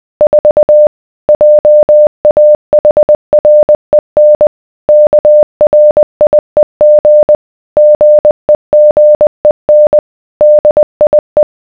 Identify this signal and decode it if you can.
Morse code, heard as a keyed tone, about 20 wpm, 600 Hz. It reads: 4 JAHREN KRIEG GEGEN DIE